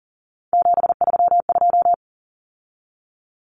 Morse code, heard as a keyed tone, about 38 wpm, 700 Hz. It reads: 732
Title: Common Cut Numbers - 3-Digit 40wpm